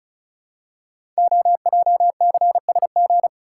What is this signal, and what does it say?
Morse code, heard as a keyed tone, about 36 wpm, 690 Hz. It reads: OJCSG
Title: Code Groups - 5 Letters - 35wpm